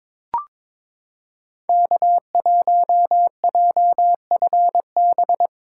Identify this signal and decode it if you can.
Morse code, heard as a keyed tone, about 22 wpm, 700 Hz. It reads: K1JFB
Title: USA Callsign DE Callsign 22wpm